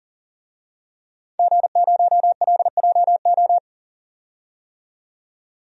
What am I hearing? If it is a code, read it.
Morse code, heard as a keyed tone, about 40 wpm, 700 Hz. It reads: G0LJO
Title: DXCC-100 Call Signs - 5 Characters - 40wpm